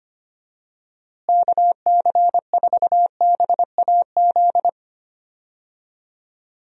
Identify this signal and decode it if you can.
Morse code, heard as a keyed tone, about 25 wpm, 700 Hz. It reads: KC4BAZ
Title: USA Call Signs - 6 Characters - Repeated 3x - 25wpm